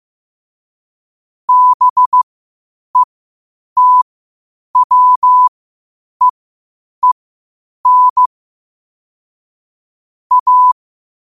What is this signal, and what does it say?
Morse code, heard as a keyed tone, about 15 wpm, spaced as for 8 wpm, 1000 Hz. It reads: BETWEEN A
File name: qotdfiles / 8wpm